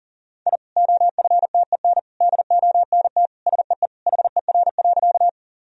Morse code, heard as a keyed tone, about 40 wpm, 700 Hz. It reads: I OFTEN DONT SEE HER.
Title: Sentences from Top 1000 Words - Review Part 1 - 40wpm